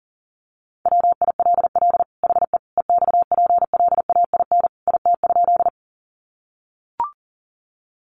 Morse code, heard as a keyed tone, about 40 wpm, 700 Hz. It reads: WILL HE EXPLAIN IT?
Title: Sentences from Top 1100 Words 40wpm